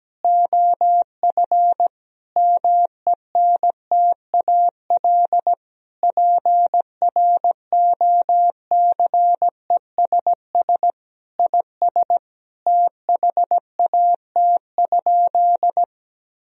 Morse code, heard as a keyed tone, about 17 wpm, 700 Hz. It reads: OF MENTAL PROCESS IS THAT?
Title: Sentences from Top 1000 Words - Review Part 4 - 17wpm